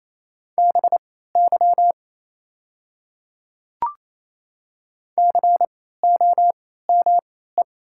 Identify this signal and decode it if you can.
Morse code, heard as a keyed tone, about 27 wpm, spaced as for 16 wpm, 690 Hz. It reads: BY COME
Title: 3x Character Spacing - Top 200 Words - 28wpm